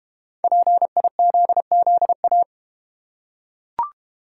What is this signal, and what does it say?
Morse code, heard as a keyed tone, about 31 wpm, 690 Hz. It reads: PIZZA
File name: Easy Words - Speed-Racing with Hints - 25wpm